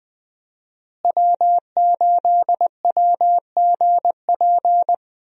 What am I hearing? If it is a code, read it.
Morse code, heard as a keyed tone, about 20 wpm, 700 Hz. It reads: W8WGP
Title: USA Call Signs - 5 Characters - Repeated 3x - 20wpm